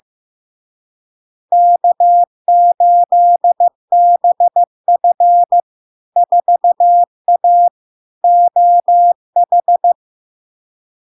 Morse code, heard as a keyed tone, about 15 wpm, 700 Hz. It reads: K8BF 4A OH
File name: LARK-FD-2022-Log-in-CW-15wpm-700-Hz